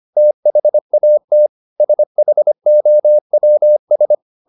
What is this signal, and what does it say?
Morse code, heard as a keyed tone, about 25 wpm, 600 Hz. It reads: THAT SHOWS
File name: the old man and the sea part 1 25wpm